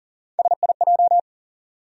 Morse code, heard as a keyed tone, about 40 wpm, 700 Hz. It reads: SIJ
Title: USA Call Sign Postfixes - 3 Letters 40wpm